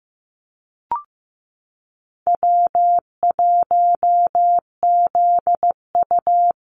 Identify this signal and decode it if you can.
Morse code, heard as a keyed tone, about 15 wpm, 700 Hz. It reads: W1ZU
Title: NA SOTA Activator Callsigns 15wpm